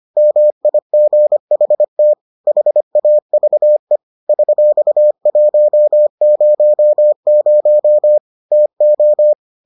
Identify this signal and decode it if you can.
Morse code, heard as a keyed tone, about 25 wpm, 600 Hz. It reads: MIGHT HAVE $100 TO